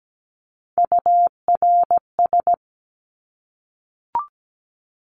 Morse code, heard as a keyed tone, about 17 wpm, 700 Hz. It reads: URS
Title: Abbreviations 17wpm